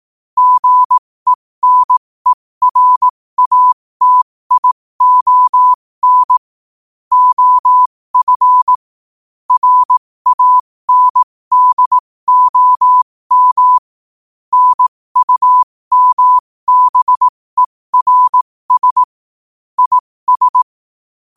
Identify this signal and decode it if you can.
Morse code, heard as a keyed tone, about 18 wpm, spaced as for 15 wpm, 1000 Hz. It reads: GENERATION OF RANDOM NUMBERS IS
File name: qotdfiles / 15wpm